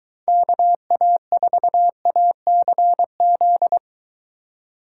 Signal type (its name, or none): Morse code